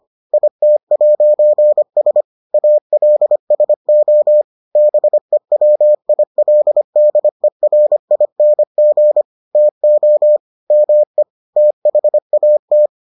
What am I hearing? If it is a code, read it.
Morse code, heard as a keyed tone, about 25 wpm, 600 Hz. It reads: IT'S ALSO BEWILDERING TO ME THAT